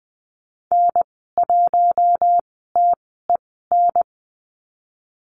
Morse code, heard as a keyed tone, about 19 wpm, spaced as for 14 wpm, 700 Hz. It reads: N1TEN
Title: 055 Lesson 10g - Letter r - Call Signs 20wpm